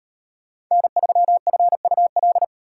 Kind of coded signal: Morse code

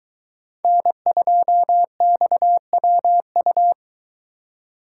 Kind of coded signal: Morse code

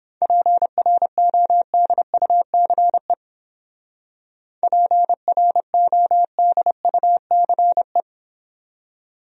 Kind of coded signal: Morse code

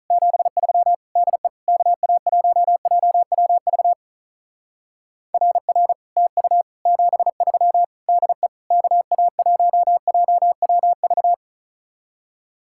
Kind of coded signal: Morse code

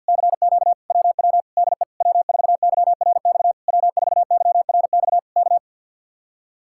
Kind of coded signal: Morse code

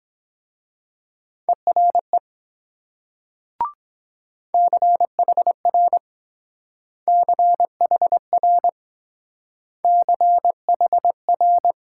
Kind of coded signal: Morse code

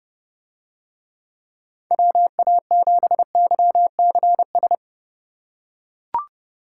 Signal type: Morse code

